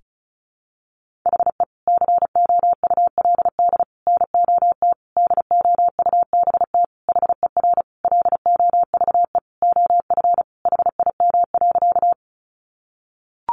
Morse code, heard as a keyed tone, about 34 wpm, 700 Hz. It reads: HE COULD NOT DOUBT HER LOVE OF HIM.
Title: Sentences from Top 1500 Words 35wpm